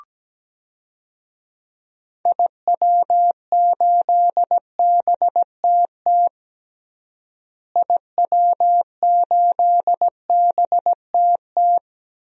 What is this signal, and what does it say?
Morse code, heard as a keyed tone, about 17 wpm, 700 Hz. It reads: IW8BTT IW8BTT
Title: DXCC-100 Call Signs - 6 Characters - Repeated 3x - 17wpm